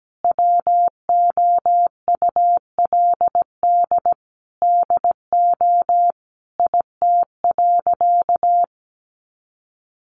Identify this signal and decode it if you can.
Morse code, heard as a keyed tone, about 17 wpm, 700 Hz. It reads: WOULD DO IT.
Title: Sentences from Top 1500 Words 17wpm